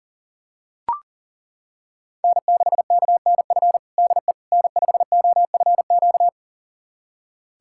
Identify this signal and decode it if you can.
Morse code, heard as a keyed tone, about 40 wpm, 700 Hz. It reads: N6KNF DE N5OFQ
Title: USA Callsign DE Callsign - 5 Characters - 40wpm